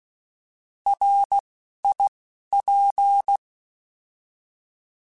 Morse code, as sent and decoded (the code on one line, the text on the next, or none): .-. / .. / .--.
R I P